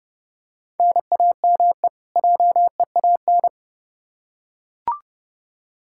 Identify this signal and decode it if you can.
Morse code, heard as a keyed tone, about 30 wpm, 700 Hz. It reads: NAME JEAN
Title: Practice QSO - Exchange 1 - Beginning - 30wpm